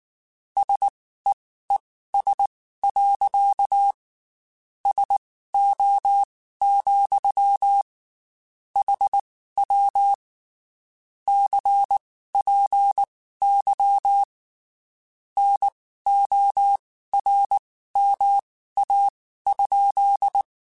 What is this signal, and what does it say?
Morse code, heard as a keyed tone, about 19 wpm, spaced as for 14 wpm, 800 Hz. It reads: SEES. SO, HW CPY NORMA?
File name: QSOs / QSOs-mp3s / WD0FBT 1@18